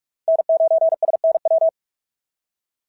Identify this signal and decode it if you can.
Morse code, heard as a keyed tone, about 46 wpm, 650 Hz. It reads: N9SNW
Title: SOTA QSO Practice 45wpm